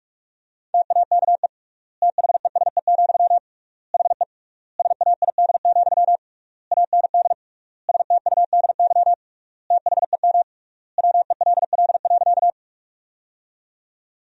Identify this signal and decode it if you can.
Morse code, heard as a keyed tone, about 42 wpm, 700 Hz. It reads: TAKE THESE, HE SAID, AND STUDY THEM WELL.
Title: 3x Word-Spacing - Sentences from Top 300 Words - 45wpm